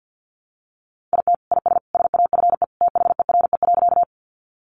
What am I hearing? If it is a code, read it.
Morse code, heard as a keyed tone, about 50 wpm, 700 Hz. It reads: IT IS SAFE THERE.
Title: Easy Phrases 50wpm